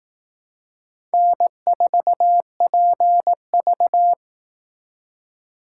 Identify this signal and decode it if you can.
Morse code, heard as a keyed tone, about 18 wpm, 700 Hz. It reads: N4PV
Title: Speed Racing - USA 1x2 Call Signs 15wpm